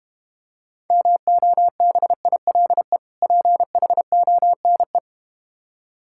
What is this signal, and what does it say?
Morse code, heard as a keyed tone, about 32 wpm, 700 Hz. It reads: MOBILE PHONE